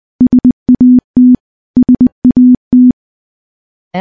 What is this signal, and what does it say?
Morse code, heard as a keyed tone, about 21 wpm, 250 Hz. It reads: SAT SAT
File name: morse / morse-training-20WPM